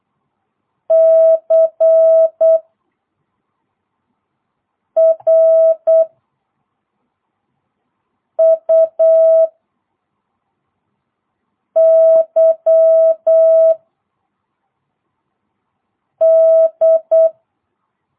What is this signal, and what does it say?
Morse code, heard as a keyed tone, about 8 wpm, 650 Hz. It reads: C R U Y D